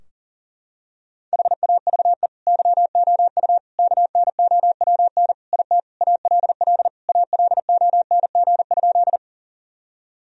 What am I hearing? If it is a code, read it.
Morse code, heard as a keyed tone, about 37 wpm, 700 Hz. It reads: HAVE YOU KNOWN IT ALL ALONG?
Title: Sentences from Top 1000 Words - Review Part 3 - 40wpm